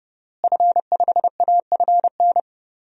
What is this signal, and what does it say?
Morse code, heard as a keyed tone, about 30 wpm, 700 Hz. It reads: F5AFN
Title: DXCC-100 Call Signs - 5 Characters - Repeated 3x - 30wpm